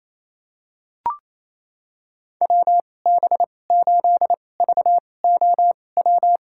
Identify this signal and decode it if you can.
Morse code, heard as a keyed tone, about 28 wpm, spaced as for 20 wpm, 700 Hz. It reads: WB8VOW
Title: USA Call Signs - 6 Characters - 2x Character Spacing - 28wpm